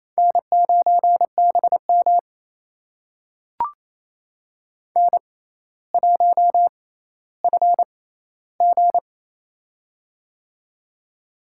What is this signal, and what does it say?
Morse code, heard as a keyed tone, about 28 wpm, spaced as for 14 wpm, 700 Hz. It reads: N9BM N1FG
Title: USA Call Signs - 4 Characters - 6x Character Spacing - 28wpm